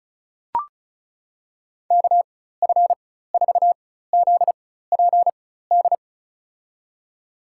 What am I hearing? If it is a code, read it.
Morse code, heard as a keyed tone, about 36 wpm, spaced as for 16 wpm, 700 Hz. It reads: KF4ZPD
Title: USA Call Signs - 6 Characters - 4x Character Spacing - 35wpm